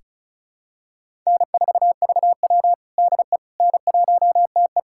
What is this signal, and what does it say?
Morse code, heard as a keyed tone, about 36 wpm, 700 Hz. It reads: N4VW DE N1TE
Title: USA Callsign DE Callsign - 4 Characters - 35wpm